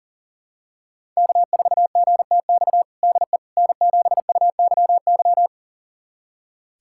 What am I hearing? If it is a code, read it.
Morse code, heard as a keyed tone, about 40 wpm, 700 Hz. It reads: K4GTX DE N7UYY
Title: USA Callsign DE Callsign 40wpm